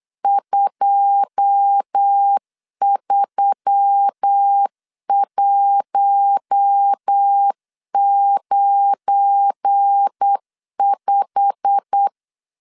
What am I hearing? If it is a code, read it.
Morse code, heard as a keyed tone, about 8 wpm, 800 Hz. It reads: 23195
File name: LEZIONI / 9 2 8wpm 3437